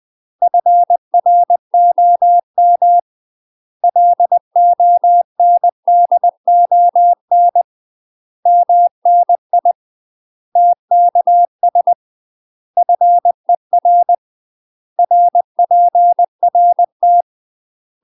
Morse code, heard as a keyed tone, about 20 wpm, 700 Hz. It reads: FROM LONDON MNI TKS FER RPRT
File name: UK MASTER QSOs TRAINING / 20WPM / UK MASTER QSOs 0065